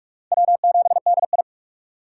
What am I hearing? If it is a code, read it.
Morse code, heard as a keyed tone, about 46 wpm, 700 Hz. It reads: W7DI
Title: USA Call Signs - 4 Characters - 3x Character Spacing - 45wpm